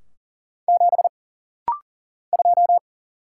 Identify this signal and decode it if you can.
Morse code, heard as a keyed tone, about 40 wpm, 700 Hz. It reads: 7 2
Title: Single Letter-Number - Rapid-Fire 40wpm